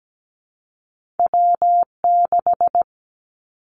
Morse code, heard as a keyed tone, about 17 wpm, 700 Hz. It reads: W6
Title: Two Challenging Letter-Number Combinations 17wpm